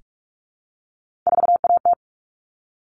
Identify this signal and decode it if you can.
Morse code, heard as a keyed tone, about 44 wpm, 700 Hz. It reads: 4AT
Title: All Cut Numbers - 3-Digit 45wpm